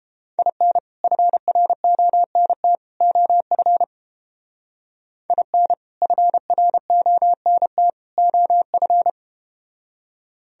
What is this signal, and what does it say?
Morse code, heard as a keyed tone, about 32 wpm, 700 Hz. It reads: IN FRONT OF IN FRONT OF